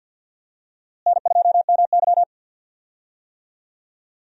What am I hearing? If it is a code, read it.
Morse code, heard as a keyed tone, about 52 wpm, 700 Hz. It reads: N2MY
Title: USA Call Signs - 4 Characters - Repeated 2x - 50wpm